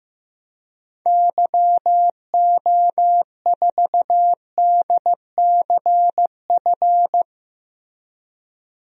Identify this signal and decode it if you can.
Morse code, heard as a keyed tone, about 15 wpm, 700 Hz. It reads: YO4DCF
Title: DXCC-100 Call Signs - 6 Characters - Repeated 2x - 15wpm